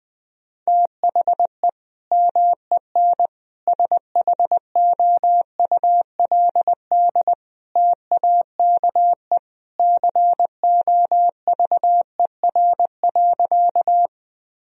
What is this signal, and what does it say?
Morse code, heard as a keyed tone, about 20 wpm, 700 Hz. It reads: THE MEN SHOULD TAKE COVER.
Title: Sentences from Top 1000 Words - Review Part 1 - 20wpm